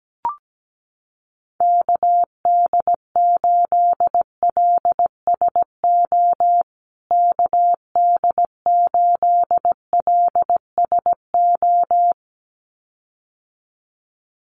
Morse code, heard as a keyed tone, about 17 wpm, 700 Hz. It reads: KD8LSO KD8LSO
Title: USA Call Signs Repeated 2x 17wpm